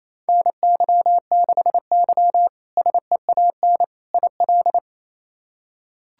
Morse code, heard as a keyed tone, about 28 wpm, 700 Hz. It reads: NY6Y SEAN IL
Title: K1USN SST QSO Practice - 28wpm